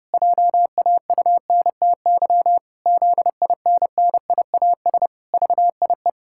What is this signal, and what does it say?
Morse code, heard as a keyed tone, about 30 wpm, 700 Hz. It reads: JAUNTY ZINNIAS VIE
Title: Pangram Challenge v2 - 30wpm